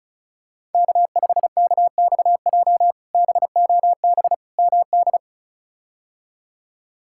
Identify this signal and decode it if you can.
Morse code, heard as a keyed tone, about 36 wpm, 700 Hz. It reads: K5KXJ BOB MD
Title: K1USN SST QSO Practice - 35wpm